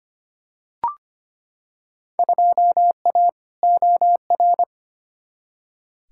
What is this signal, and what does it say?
Morse code, heard as a keyed tone, about 25 wpm, 700 Hz. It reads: 2A OR
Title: Field Day Exchanges - 25wpm